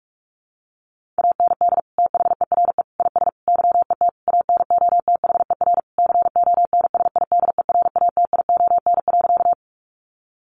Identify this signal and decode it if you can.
Morse code, heard as a keyed tone, about 44 wpm, 700 Hz. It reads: AND THERE IS YET ANOTHER CONSIDERATION.
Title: Sentences from Top 1500 Words 45wpm